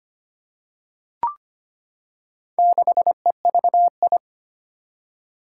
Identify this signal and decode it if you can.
Morse code, heard as a keyed tone, about 25 wpm, 700 Hz. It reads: 6EVI
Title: Four Challenging Letter-Number Combinations 25wpm